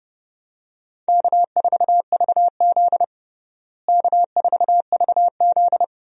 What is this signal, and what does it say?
Morse code, heard as a keyed tone, about 30 wpm, 700 Hz. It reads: K4VZ K4VZ
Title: USA Call Signs - 4 Characters - Repeated 3x - 30wpm